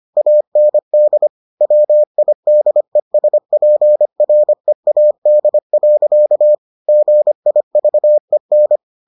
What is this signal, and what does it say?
Morse code, heard as a keyed tone, about 25 wpm, 600 Hz. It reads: AND WIDESPREAD. GIVEN